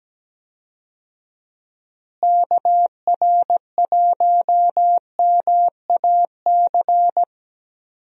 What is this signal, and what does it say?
Morse code, heard as a keyed tone, about 17 wpm, 700 Hz. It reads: KR1MAC
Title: USA Call Signs - 6 Characters - Repeated 3x - 17wpm